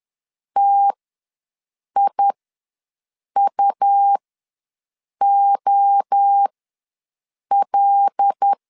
Morse code, heard as a keyed tone, about 10 wpm, spaced as for 6 wpm, 800 Hz. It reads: TIUOL
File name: LEZIONI / 5 LBVC / 5 L ws 10wpm 3636